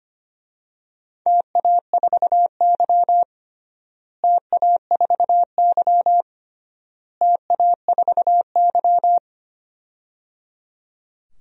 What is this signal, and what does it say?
Morse code, heard as a keyed tone, about 25 wpm, 700 Hz. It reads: TA4Y TA4Y TA4Y
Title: DXCC-100 Call Signs - 4 Characters - Repeated 3x - 25wpm